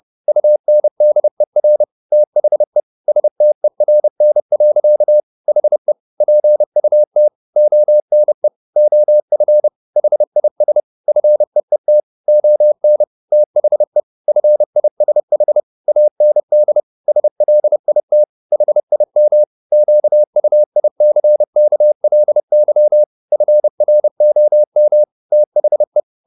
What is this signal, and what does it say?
Morse code, heard as a keyed tone, about 30 wpm, 600 Hz. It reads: UNDER THE STERN. HE PUT ONE OF HIS FEET ON THE FISH AND SLIT HIM QUICKLY FROM THE